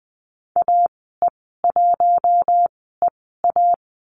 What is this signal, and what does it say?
Morse code, heard as a keyed tone, about 20 wpm, spaced as for 15 wpm, 700 Hz. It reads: AE1EA
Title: 042 Lesson 8g - Number 1 - Call Signs 20wpm